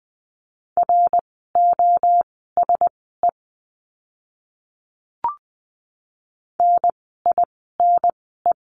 Morse code, heard as a keyed tone, about 20 wpm, spaced as for 15 wpm, 700 Hz. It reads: ROSE NINE
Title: 054 Lesson 10f - Letter r - Words 20wpm